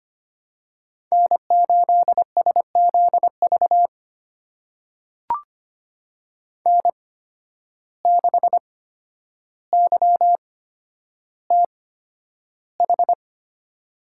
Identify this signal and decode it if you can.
Morse code, heard as a keyed tone, about 25 wpm, spaced as for 11 wpm, 700 Hz. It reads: N8SZV N6YTH
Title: USA Call Signs - 5 Characters - 8x Character Spacing - 25wpm